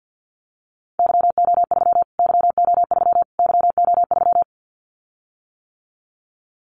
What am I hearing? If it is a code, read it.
Morse code, heard as a keyed tone, about 50 wpm, 700 Hz. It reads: YO3 YO3 YO3
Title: Callsign Prefixes for Top 25 DXCC - Repeated 3x - 50wpm